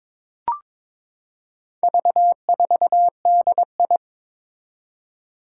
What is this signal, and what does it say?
Morse code, heard as a keyed tone, about 21 wpm, 700 Hz. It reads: V4DI
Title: Four Challenging Letter-Number Combinations 22wpm